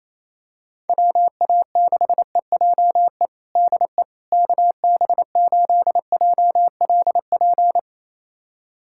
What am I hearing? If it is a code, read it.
Morse code, heard as a keyed tone, about 28 wpm, 700 Hz. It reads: WA6EJE DE KB8JLP
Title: USA Callsign DE Callsign - 6 Characters - 28wpm